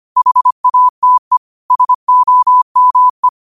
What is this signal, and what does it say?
Morse code, heard as a keyed tone, about 25 wpm, 1000 Hz. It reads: HATE SOME
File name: qotdfiles / 25wpm